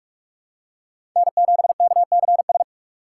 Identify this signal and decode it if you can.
Morse code, heard as a keyed tone, about 46 wpm, 700 Hz. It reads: N7KCS
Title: USA Call Signs - 5 Characters - 2x Character Spacing - 45wpm